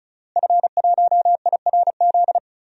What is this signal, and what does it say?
Morse code, heard as a keyed tone, about 34 wpm, 700 Hz. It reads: F1IRZ
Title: DXCC-100 Call Signs - 5 Characters - Repeated 3x - 35wpm